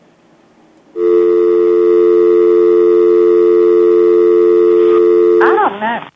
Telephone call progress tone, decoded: dial tone